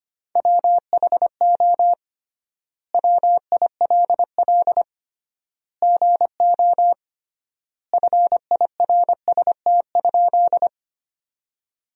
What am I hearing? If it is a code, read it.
Morse code, heard as a keyed tone, about 25 wpm, 700 Hz. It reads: WHO WILL GO FIRST?
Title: 3x Word-Spacing - Sentences from Top 100 Words - 25wpm